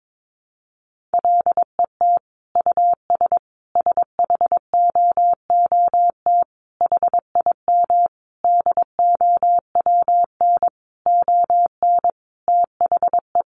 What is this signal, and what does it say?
Morse code, heard as a keyed tone, about 22 wpm, 700 Hz. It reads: LET US SHOOT HIM DOWN ON THE
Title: Sentences from Top 1200 Words 22wpm